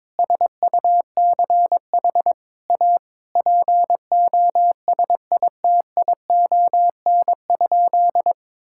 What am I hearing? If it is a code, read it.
Morse code, heard as a keyed tone, about 21 wpm, 700 Hz. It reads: SUCH A POSITION?
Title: Sentences from Top 1000 Words - Review Part 5 - 22wpm